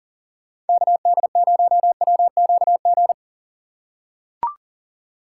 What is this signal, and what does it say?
Morse code, heard as a keyed tone, about 40 wpm, 700 Hz. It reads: KD0WQG
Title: USA Call Signs - 6 Characters - 6x Character Spacing - 40wpm